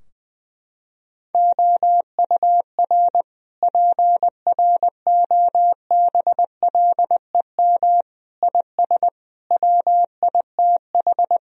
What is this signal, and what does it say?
Morse code, heard as a keyed tone, about 20 wpm, 700 Hz. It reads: OUR PROBLEM IS WITH
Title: Sentences from Top 1000 Words - Review Part 2 - 20wpm